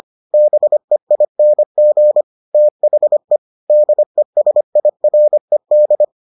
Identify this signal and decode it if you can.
Morse code, heard as a keyed tone, about 25 wpm, 600 Hz. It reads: BEING THE DESIRED